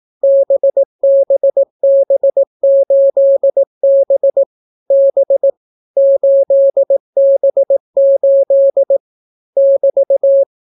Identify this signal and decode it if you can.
Morse code, heard as a keyed tone, about 18 wpm, 550 Hz. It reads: BBB8B B 8B8 <BT>